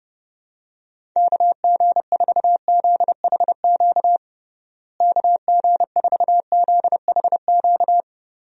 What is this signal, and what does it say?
Morse code, heard as a keyed tone, about 32 wpm, 700 Hz. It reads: KG4ZHQ KG4ZHQ
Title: USA Call Signs - 6 Characters - Repeated 3x - 30wpm